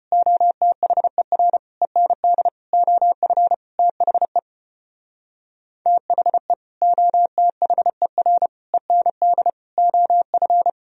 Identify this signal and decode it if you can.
Morse code, heard as a keyed tone, about 31 wpm, 700 Hz. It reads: OTHER END OF THE THE OTHER END OF
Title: Speed-Racing - Sets of 5 Words 25wpm